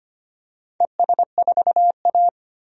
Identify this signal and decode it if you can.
Morse code, heard as a keyed tone, about 25 wpm, 700 Hz. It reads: ES4A
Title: DXCC-100 Call Signs - 4 Characters - 25wpm